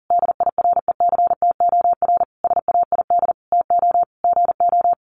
Morse code, heard as a keyed tone, about 40 wpm, 700 Hz. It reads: DIRECTOR SAID TO GO
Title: Sentences from Top 1300 Words 40wpm